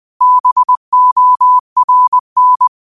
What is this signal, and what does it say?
Morse code, heard as a keyed tone, about 20 wpm, 1000 Hz. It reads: BORN
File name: qotdfiles / 20wpm